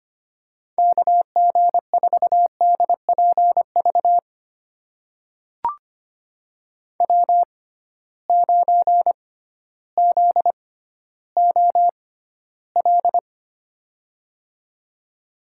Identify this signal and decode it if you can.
Morse code, heard as a keyed tone, about 25 wpm, spaced as for 13 wpm, 700 Hz. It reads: KG4DPV W9ZOL
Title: USA Call Signs - 6x Character Spacing - 25wpm